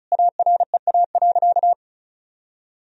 Morse code, heard as a keyed tone, about 36 wpm, 700 Hz. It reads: AREA.
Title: Sentences from Top 1000 Words - Review Part 2 - 35wpm